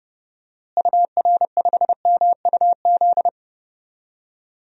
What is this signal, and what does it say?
Morse code, heard as a keyed tone, about 32 wpm, 700 Hz. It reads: UR5MUZ
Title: DXCC-100 Call Signs - 6 Characters - Repeated 2x - 30wpm